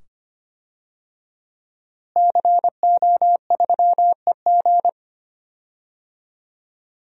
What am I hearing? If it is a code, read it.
Morse code, heard as a keyed tone, about 25 wpm, 700 Hz. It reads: CO3EG
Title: DXCC-100 Call Signs - 5 Characters - Repeated 2x - 25wpm